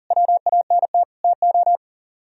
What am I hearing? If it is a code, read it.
Morse code, heard as a keyed tone, about 51 wpm, 700 Hz. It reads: WANT TO